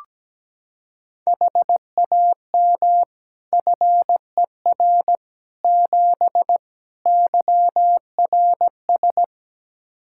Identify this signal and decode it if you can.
Morse code, heard as a keyed tone, about 17 wpm, 700 Hz. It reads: HAM FER 7 YRS
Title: Practice QSO - Exchange 3 - More Info - 17wpm